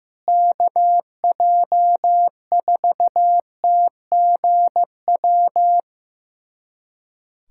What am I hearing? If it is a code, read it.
Morse code, heard as a keyed tone, about 15 wpm, 700 Hz. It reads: KJ4TGW
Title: USA Callsign DE Callsign - 6 Characters - 15wpm